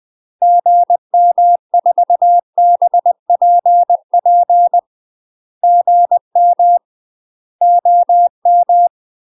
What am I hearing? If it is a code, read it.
Morse code, heard as a keyed tone, about 20 wpm, 700 Hz. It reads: GM4BPP GM OM
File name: UK MASTER QSOs TRAINING / 20WPM / UK MASTER QSOs 0017